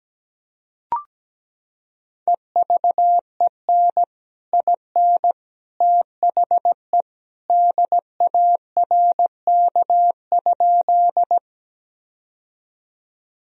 Morse code, heard as a keyed tone, about 17 wpm, 700 Hz. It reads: EVEN IN THE DARK?
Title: Sentences from Top 1000 Words - Review Part 1 - 17wpm